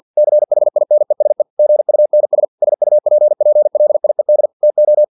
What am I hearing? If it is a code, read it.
Morse code, heard as a keyed tone, about 50 wpm, 600 Hz. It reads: CHINESE GUNS SUPPLIED TO
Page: Never 49wpm